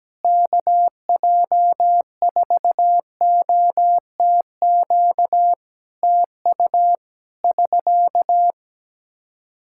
Morse code, heard as a keyed tone, about 17 wpm, 700 Hz. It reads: KJ4OTQ TU <SK>
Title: Practice QSO - Exchange 4 - Ending - 17wpm